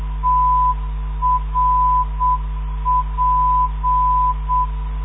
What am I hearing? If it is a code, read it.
Morse code, heard as a keyed tone, about 7 wpm, 1000 Hz. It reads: TRP
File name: ndb countryes / zvoki mp3 / trp 317.5khz mp3